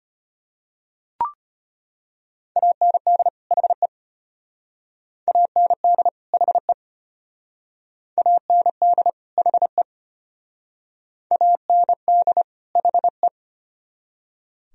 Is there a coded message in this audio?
Yes: Morse code from 2.6 to 13.3 s, heard as a keyed tone, about 32 wpm, 700 Hz. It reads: AND HE AND HE AND HE AND HE